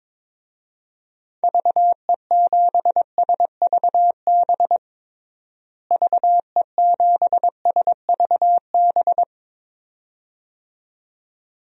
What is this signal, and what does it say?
Morse code, heard as a keyed tone, about 22 wpm, 700 Hz. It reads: VE7SVB VE7SVB
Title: DXCC-100 Call Signs - 6 Characters - Repeated 2x - 22wpm